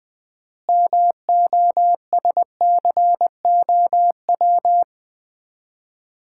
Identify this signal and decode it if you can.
Morse code, heard as a keyed tone, about 21 wpm, 700 Hz. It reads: MOSCOW
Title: Common QSO Phrases - 4x Word-Spacing - 20wpm